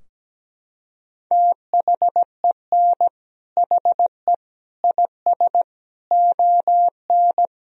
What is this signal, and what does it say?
Morse code, heard as a keyed tone, about 17 wpm, 700 Hz. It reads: THEN HE IS ON
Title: Sentences from Top 1000 Words - Review Part 4 - 17wpm